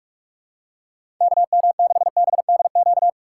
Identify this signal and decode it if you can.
Morse code, heard as a keyed tone, about 46 wpm, 700 Hz. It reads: KM6BDQ
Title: USA Call Signs - 4x Character Spacing - 45wpm